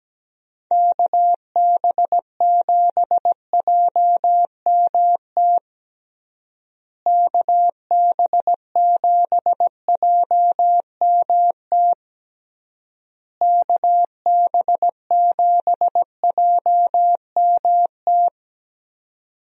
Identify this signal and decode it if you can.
Morse code, heard as a keyed tone, about 17 wpm, 700 Hz. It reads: KB7JMT KB7JMT KB7JMT
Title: USA Call Signs - 6 Characters - Repeated 3x - 17wpm